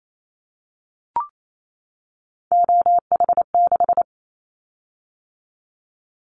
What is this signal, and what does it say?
Morse code, heard as a keyed tone, about 28 wpm, 700 Hz. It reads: OH6
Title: Callsign Prefixes for Top 25 DXCC - 28wpm